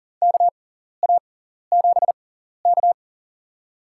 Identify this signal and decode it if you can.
Morse code, heard as a keyed tone, about 40 wpm, spaced as for 14 wpm, 700 Hz. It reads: KA7K
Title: USA Call Signs - 4 Characters - 6x Character Spacing - 40wpm